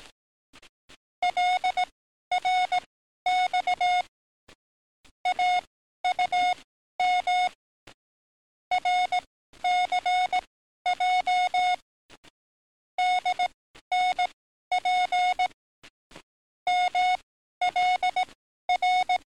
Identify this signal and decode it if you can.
Morse code, heard as a keyed tone, about 18 wpm, spaced as for 12 wpm, 730 Hz. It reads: LRX AUM RCJ DNP MLR
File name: training / NAVPERS 11049to11060 / NAVPERS 11052 RA RB